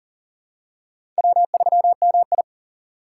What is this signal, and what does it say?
Morse code, heard as a keyed tone, about 38 wpm, 700 Hz. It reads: W3MI
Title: USA Call Signs - 4 Characters - 2x Character Spacing - 40wpm